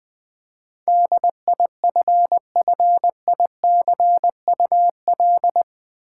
Morse code, heard as a keyed tone, about 20 wpm, 700 Hz. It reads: DIFFICUL
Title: Build A Word - 4 Letters At A Time - 10-Letter Words - 20wpm